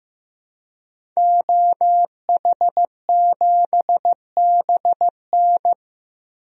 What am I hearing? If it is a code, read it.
Morse code, heard as a keyed tone, about 15 wpm, 700 Hz. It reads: OH7BN
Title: DXCC-100 Call Signs - 5 Characters - Repeated 2x - 15wpm